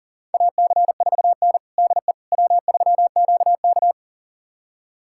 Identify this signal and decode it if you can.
Morse code, heard as a keyed tone, about 40 wpm, 700 Hz. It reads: AC4N DE W3QK
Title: USA Callsign DE Callsign - 4 Characters - 40wpm